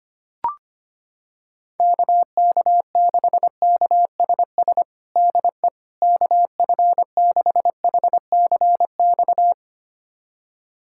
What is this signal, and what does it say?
Morse code, heard as a keyed tone, about 25 wpm, 700 Hz. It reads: KK6KSS DE KF6HCX
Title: USA Callsign DE Callsign - 6 Characters - 25wpm